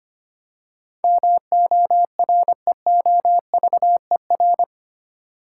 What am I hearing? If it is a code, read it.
Morse code, heard as a keyed tone, about 25 wpm, 700 Hz. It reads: MOREOVER
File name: 8 Letter Words - First 3 Letters Given - 25wpm